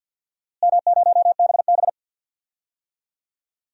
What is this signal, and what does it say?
Morse code, heard as a keyed tone, about 49 wpm, 700 Hz. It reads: M0BB
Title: DXCC-100 Call Signs - 4 Characters - Repeated 3x - 50wpm